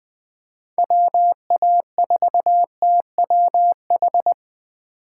Morse code, heard as a keyed tone, about 20 wpm, 700 Hz. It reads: WA4TWH
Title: Speed Racing - USA Call Signs 17wpm